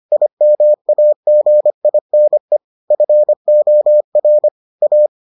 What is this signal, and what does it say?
Morse code, heard as a keyed tone, about 25 wpm, 600 Hz. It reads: IMAGINE FOR A